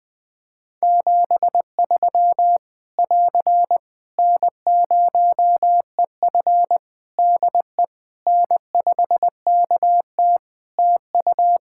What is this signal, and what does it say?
Morse code, heard as a keyed tone, about 19 wpm, 700 Hz. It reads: 73 <AR> N0EF DE N5KT TU
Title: Practice QSO - Exchange 4 - Ending - 20wpm